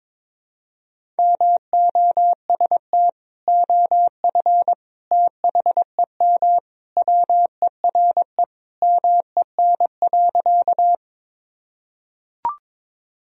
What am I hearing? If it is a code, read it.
Morse code, heard as a keyed tone, about 22 wpm, 700 Hz. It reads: MOST OF THEM WERE MEN.
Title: Sentences from Top 1000 Words - Review Part 1 - 22wpm